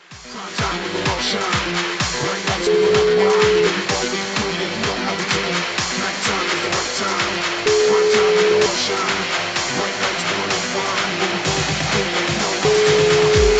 Ringback tone (Europe, 425 Hz), with noise.